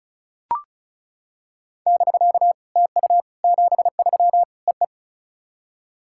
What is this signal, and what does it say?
Morse code, heard as a keyed tone, about 36 wpm, 700 Hz. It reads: <BK> TU 73 EE
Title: SOTA QSO Practice 35wpm